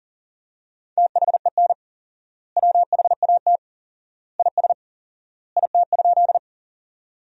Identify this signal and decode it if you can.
Morse code, heard as a keyed tone, about 40 wpm, 700 Hz. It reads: THEN WHAT IS IT?
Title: 4x Word-Spacing - Sentences from Top 100 Words - 40wpm